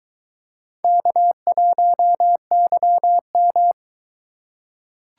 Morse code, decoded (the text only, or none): K1YM